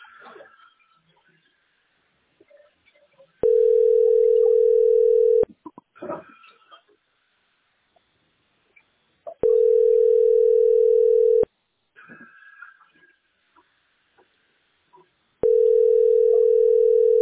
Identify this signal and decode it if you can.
Ringback tone (North America), 440 + 480 Hz.